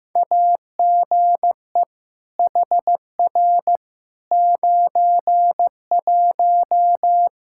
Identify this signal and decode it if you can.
Morse code, heard as a keyed tone, about 15 wpm, 700 Hz. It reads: AGE HR 91
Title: Practice QSO - Exchange 3 - More Info - 15wpm